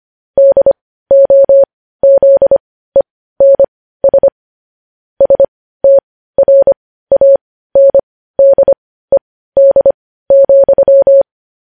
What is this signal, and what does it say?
Morse code, heard as a keyed tone, about 25 wpm, spaced as for 15 wpm, 550 Hz. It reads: DOZENS STRANDED,
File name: podcast / News headlines-1x-25x15